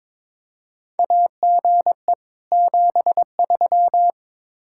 Morse code, heard as a keyed tone, about 22 wpm, 700 Hz. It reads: AGE 73
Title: Practice QSO - Exchange 3 - More Info - 22wpm